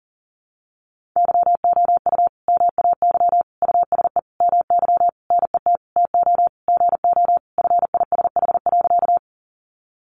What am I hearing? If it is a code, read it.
Morse code, heard as a keyed tone, about 40 wpm, 700 Hz. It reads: YOU MAY USE MY NET TO GO FISH.
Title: Sentences from Top 1400 Words 40wpm